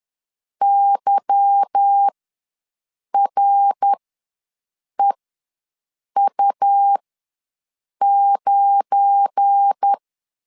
Morse code, heard as a keyed tone, about 10 wpm, spaced as for 6 wpm, 800 Hz. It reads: YREU9